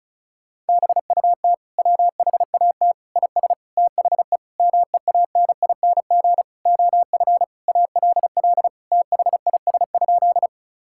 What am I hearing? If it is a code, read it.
Morse code, heard as a keyed tone, about 36 wpm, 700 Hz. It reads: BUT WHAT IS THE MEANING OF ALL THIS?
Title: Sentences from Top 1000 Words - Review Part 6 - 35wpm